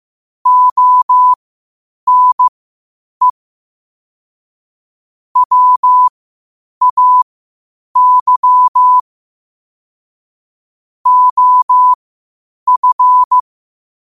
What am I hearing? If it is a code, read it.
Morse code, heard as a keyed tone, about 15 wpm, spaced as for 8 wpm, 1000 Hz. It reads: ONE WAY OF
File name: qotdfiles / 8wpm